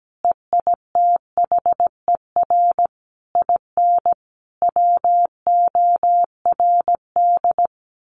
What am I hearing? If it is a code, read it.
Morse code, heard as a keyed tone, about 17 wpm, 700 Hz. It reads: EITHER IN WORD
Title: Sentences from Top 1200 Words 17wpm